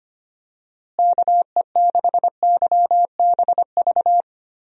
Morse code, heard as a keyed tone, about 25 wpm, 700 Hz. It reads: KE6YBV
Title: USA Call Signs - 6 Characters - Repeated 3x - 25wpm